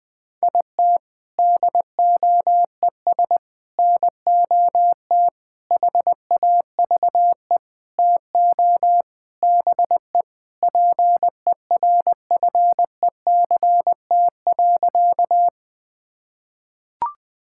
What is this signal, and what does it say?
Morse code, heard as a keyed tone, about 20 wpm, 700 Hz. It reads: IT DOES NOT HAVE TO BE PERFECT.
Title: Sentences with 5-Word Phrases - Spoken Beginning - 20wpm